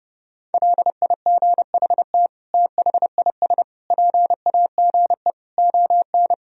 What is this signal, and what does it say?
Morse code, heard as a keyed tone, about 32 wpm, 700 Hz. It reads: LIGHT THIS PAGE ON
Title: Sentences from Top 1000 Words - Review Part 1 - 30wpm